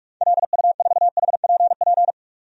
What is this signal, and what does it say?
Morse code, heard as a keyed tone, about 46 wpm, 700 Hz. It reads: LU4HPP
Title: DXCC-100 Call Signs - 6 Characters - 45wpm